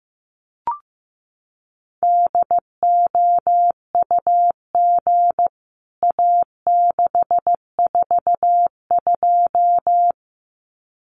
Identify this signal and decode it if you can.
Morse code, heard as a keyed tone, about 15 wpm, 700 Hz. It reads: DOUG A642
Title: CWT Exchange 15wpm